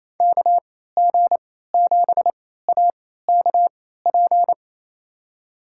Morse code, heard as a keyed tone, about 28 wpm, spaced as for 16 wpm, 700 Hz. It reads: KG7AKP
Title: USA Call Signs - 3x Character Spacing - 28wpm